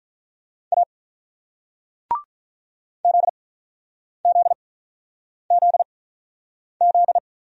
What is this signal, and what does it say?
Morse code, heard as a keyed tone, about 48 wpm, 700 Hz. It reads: A Z Z Z Z